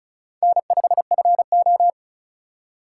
Morse code, heard as a keyed tone, about 36 wpm, 700 Hz. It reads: N5FO
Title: USA Call Signs - 4 Characters - 4x Character Spacing - 35wpm